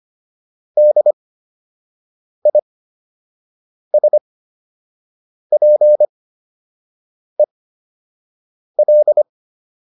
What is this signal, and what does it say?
Morse code, heard as a keyed tone, about 25 wpm, spaced as for 6 wpm, 600 Hz. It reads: DISPEL